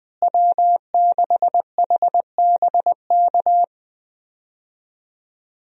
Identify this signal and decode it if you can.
Morse code, heard as a keyed tone, about 20 wpm, 700 Hz. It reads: W6HBK
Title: USA Call Signs - 5 Characters - Repeated 3x - 20wpm